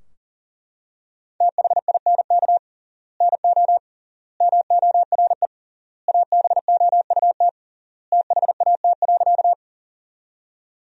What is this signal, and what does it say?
Morse code, heard as a keyed tone, about 40 wpm, 700 Hz. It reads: THINK NO MORE ABOUT THAT.
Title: 3x Word-Spacing - Sentences from Top 200 Words - 40wpm